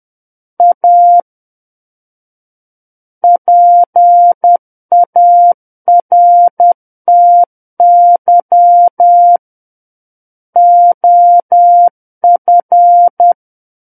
Morse code, heard as a keyed tone, about 10 wpm, 700 Hz. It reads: A PARTY OF
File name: Book One / 01-Chapter 1 - 10 WPM